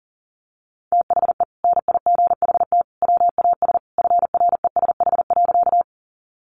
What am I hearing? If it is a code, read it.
Morse code, heard as a keyed tone, about 40 wpm, 700 Hz. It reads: THE NIGHT WAS FRESH.
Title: Sentences from Top 1200 Words 40wpm